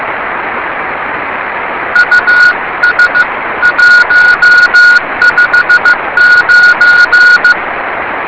Morse code, heard as a keyed tone, about 15 wpm, 1450 Hz. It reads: US 159
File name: rs30 cw